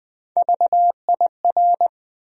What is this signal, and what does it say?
Morse code, heard as a keyed tone, about 20 wpm, 700 Hz. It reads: VIR
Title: USA Call Sign Postfixes - 3 Letters 20wpm